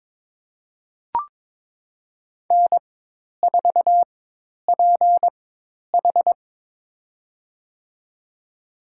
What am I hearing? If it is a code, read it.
Morse code, heard as a keyed tone, about 21 wpm, spaced as for 10 wpm, 700 Hz. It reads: N4PH
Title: USA Call Signs - 4 Characters - 4x Character Spacing - 22wpm